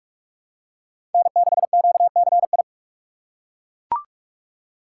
Morse code, heard as a keyed tone, about 41 wpm, 700 Hz. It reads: N6QCI
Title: USA Call Signs - 4x Character Spacing - 45wpm